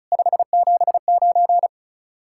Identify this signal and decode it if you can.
Morse code, heard as a keyed tone, about 46 wpm, 700 Hz. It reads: 579